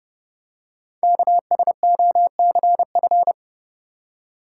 Morse code, heard as a keyed tone, about 30 wpm, 700 Hz. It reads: KSOCF
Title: Code Groups - 5 Letter:Numbers - 30wpm